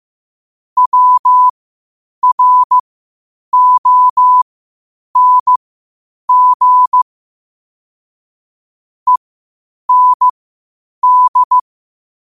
Morse code, heard as a keyed tone, about 15 wpm, spaced as for 8 wpm, 1000 Hz. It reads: WRONG END
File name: qotdfiles / 8wpm